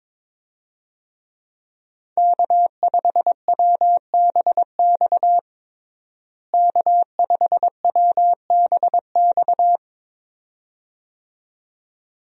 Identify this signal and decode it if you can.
Morse code, heard as a keyed tone, about 22 wpm, 700 Hz. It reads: K5WBX K5WBX
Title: USA Call Signs - 5 Characters - Repeated 2x - 22wpm